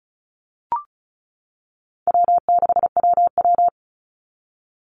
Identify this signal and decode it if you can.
Morse code, heard as a keyed tone, about 34 wpm, 700 Hz. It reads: W6WW
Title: NA SOTA Chaser Callsigns 35wpm